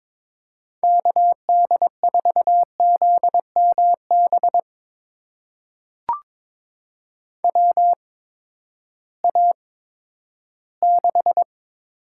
Morse code, heard as a keyed tone, about 23 wpm, spaced as for 13 wpm, 700 Hz. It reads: KD4ZMB WA6
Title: USA Call Signs - 6 Characters - 8x Character Spacing - 22wpm